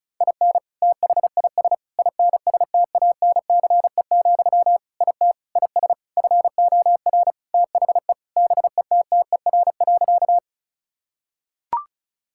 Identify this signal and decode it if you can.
Morse code, heard as a keyed tone, about 36 wpm, 700 Hz. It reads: IN THIS INSTANCE, IT IS FOR THE BETTER.
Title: Sentences from Top 1000 Words - Review Part 5 - 35wpm